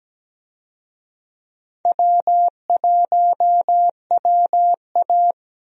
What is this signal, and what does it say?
Morse code, heard as a keyed tone, about 17 wpm, 700 Hz. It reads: W1WA
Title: USA Call Signs - 4 Characters - 17wpm